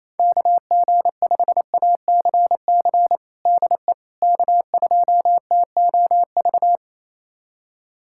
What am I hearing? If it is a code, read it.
Morse code, heard as a keyed tone, about 28 wpm, 700 Hz. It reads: KG5ACC DE K2TOV
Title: USA Callsign DE Callsign 28wpm